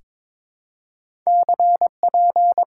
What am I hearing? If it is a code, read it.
Morse code, heard as a keyed tone, about 22 wpm, 700 Hz. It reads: CP
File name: Top 90 QSO Elements - 6x Character Spacing - 22wpm